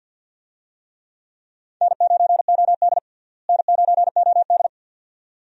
Morse code, heard as a keyed tone, about 50 wpm, 700 Hz. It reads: N9OD N9OD
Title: USA Call Signs - 4 Characters - Repeated 2x - 50wpm